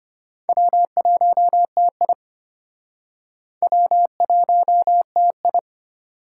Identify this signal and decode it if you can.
Morse code, heard as a keyed tone, about 33 wpm, 700 Hz. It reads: W1TI W1TI